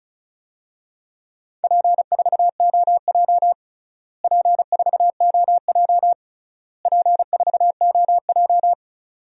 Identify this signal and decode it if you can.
Morse code, heard as a keyed tone, about 34 wpm, 700 Hz. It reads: P4OJ P4OJ P4OJ
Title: DXCC-100 Call Signs - 4 Characters - Repeated 3x - 35wpm